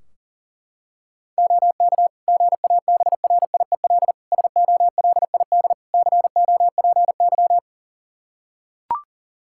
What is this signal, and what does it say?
Morse code, heard as a keyed tone, about 40 wpm, 700 Hz. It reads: OK GABRIEL SOLID COPY
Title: Practice QSO - Exchange 3 - More Info - 40wpm